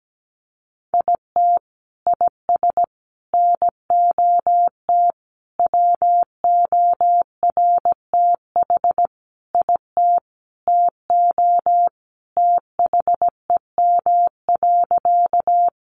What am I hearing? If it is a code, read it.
Morse code, heard as a keyed tone, about 17 wpm, 700 Hz. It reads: IT IS NOT WORTH IT TO THEM.
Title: Sentences from Top 800 Words 17wpm